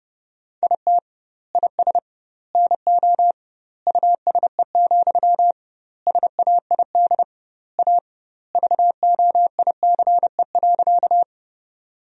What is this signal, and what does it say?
Morse code, heard as a keyed tone, about 30 wpm, 700 Hz. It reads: IT IS NO USE, SAID A VOICE.
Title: 2x Word-Spacing - Sentences from Top 500 Words - 30wpm